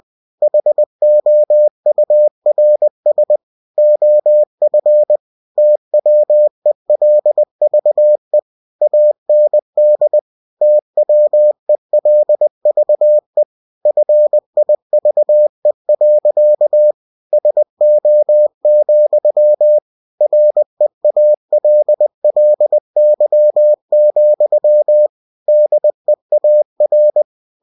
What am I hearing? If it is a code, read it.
Morse code, heard as a keyed tone, about 20 wpm, 600 Hz. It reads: HOURS OF TWELVE AND TWELVE FIVE. SO, REALLY, DEAR
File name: cw-books / winnie-the-pooh / winnie-20wpm-page-0024